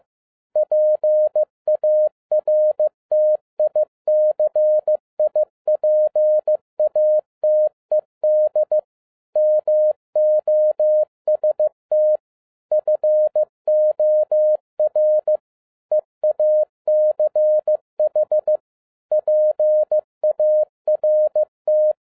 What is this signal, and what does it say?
Morse code, heard as a keyed tone, about 15 wpm, 610 Hz. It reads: PARTICIPATED MOST FOR EACH PART
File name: 2012-01-mp3 / n17-SUMMARY OF FEA NET IN 2011-15wpm